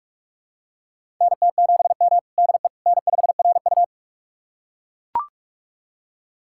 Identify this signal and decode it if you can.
Morse code, heard as a keyed tone, about 44 wpm, 690 Hz. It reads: NT7M DE N5RU
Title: USA Callsign DE Callsign - 4 Characters - 45wpm